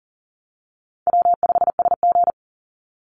Morse code, heard as a keyed tone, about 40 wpm, 700 Hz. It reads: W5SG
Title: Call Signs Repeated 3x - 40wpm